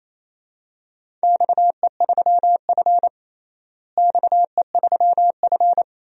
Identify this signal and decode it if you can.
Morse code, heard as a keyed tone, about 28 wpm, 700 Hz. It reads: XE3F XE3F
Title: DXCC-100 Call Signs - 4 Characters - Repeated 2x - 28wpm